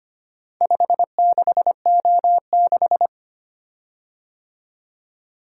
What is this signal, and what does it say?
Morse code, heard as a keyed tone, about 25 wpm, 700 Hz. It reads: 56O6
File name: practice / 25wpm / Four Letter-Number Combinations - 25wpm